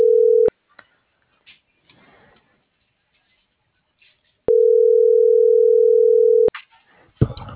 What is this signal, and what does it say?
Ringback tone (North America), 440 + 480 Hz.